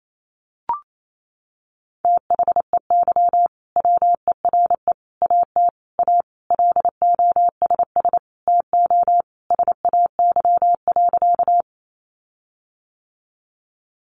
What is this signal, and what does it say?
Morse code, heard as a keyed tone, about 28 wpm, 700 Hz. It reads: THEY WERE AT A LOSS TO SAY.
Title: Sentences from Top 1400 Words 28wpm